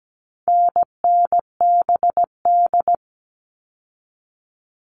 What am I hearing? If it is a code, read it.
Morse code, heard as a keyed tone, about 17 wpm, 700 Hz. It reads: NNBD
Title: All Cut Numbers - 4-Digit 17wpm